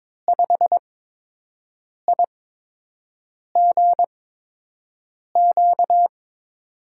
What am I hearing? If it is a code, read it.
Morse code, heard as a keyed tone, about 22 wpm, spaced as for 6 wpm, 700 Hz. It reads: 5IGQ
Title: USA Call Signs - 5 Characters - 8x Character Spacing - 22wpm